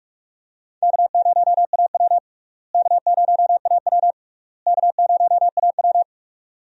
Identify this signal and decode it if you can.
Morse code, heard as a keyed tone, about 42 wpm, 700 Hz. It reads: K0AW K0AW K0AW
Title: USA Call Signs - 4 Characters - Repeated 3x - 45wpm